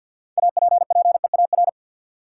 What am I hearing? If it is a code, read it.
Morse code, heard as a keyed tone, about 52 wpm, 700 Hz. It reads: APPEAR
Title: Sentences from Top 1200 Words - New Word Drill Alt - 50wpm